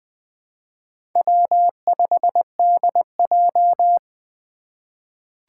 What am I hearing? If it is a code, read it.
Morse code, heard as a keyed tone, about 20 wpm, 700 Hz. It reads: W5DJ
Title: Four Challenging Letter-Number Combinations 20wpm